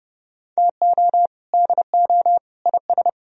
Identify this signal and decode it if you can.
Morse code, heard as a keyed tone, about 30 wpm, 700 Hz. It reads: TO DO IS